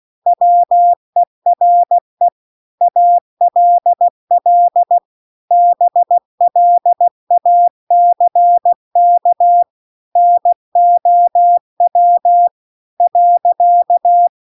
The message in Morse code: WERE ALL BLACK NOW .

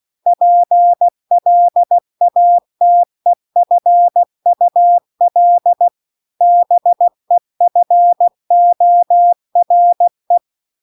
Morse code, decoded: PLATEFUL BEFORE

